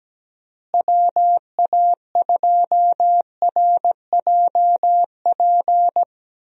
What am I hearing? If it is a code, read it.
Morse code, heard as a keyed tone, about 17 wpm, 700 Hz. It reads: WA2RJP
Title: USA Call Signs - 6 Characters - Repeated 2x - 17wpm